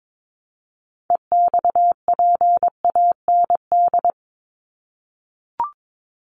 Morse code, heard as a keyed tone, about 22 wpm, 700 Hz. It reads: EXPAND
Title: Top 1251-1500 Words 22wpm